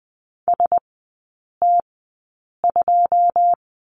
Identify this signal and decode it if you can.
Morse code, heard as a keyed tone, about 20 wpm, spaced as for 8 wpm, 700 Hz. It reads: ST2
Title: 081 Lesson 14e - Number 2 - Three Characters 20wpm